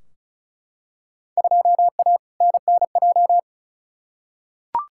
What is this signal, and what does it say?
Morse code, heard as a keyed tone, about 36 wpm, 700 Hz. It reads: 2A NNJ
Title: Field Day Exchanges - 35wpm